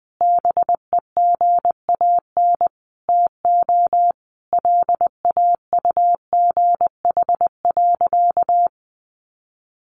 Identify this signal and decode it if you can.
Morse code, heard as a keyed tone, about 19 wpm, 700 Hz. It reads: BEGAN TO LAUGH.
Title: Sentences from Top 600 Words 20wpm